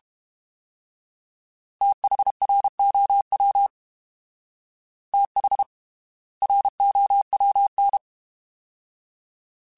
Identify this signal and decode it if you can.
Morse code, heard as a keyed tone, about 32 wpm, spaced as for 25 wpm, 800 Hz. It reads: THROW THROWN
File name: TH / TH mp3s / Fast TH @30